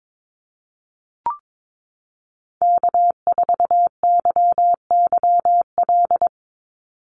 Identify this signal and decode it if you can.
Morse code, heard as a keyed tone, about 22 wpm, 700 Hz. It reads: K4YYL
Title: Call Signs Repeated 2x - 22wpm